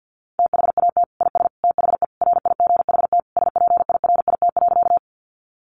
Morse code, heard as a keyed tone, about 50 wpm, 700 Hz. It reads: THAT IS THE RIGHT SPIRIT.
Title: Sentences from Top 1400 Words 50wpm